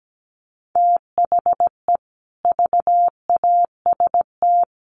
Morse code, heard as a keyed tone, about 17 wpm, 700 Hz. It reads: THE VAST
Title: Sets of 5 Words Encore 17wpm